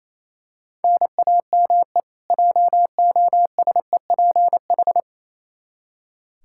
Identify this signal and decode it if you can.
Morse code, heard as a keyed tone, about 28 wpm, 700 Hz. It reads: NAME JOSEPH
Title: Practice QSO - Exchange 1 - Beginning - 28wpm